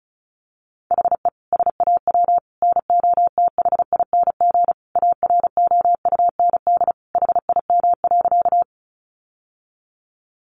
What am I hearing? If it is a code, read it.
Morse code, heard as a keyed tone, about 34 wpm, 700 Hz. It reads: HE SAW NOTHING AROUND HIM.
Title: Sentences from Top 600 Words 35wpm